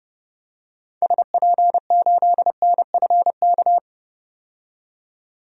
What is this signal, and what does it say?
Morse code, heard as a keyed tone, about 32 wpm, 700 Hz. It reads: SP8NFK
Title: DXCC-100 Call Signs - 6 Characters - 30wpm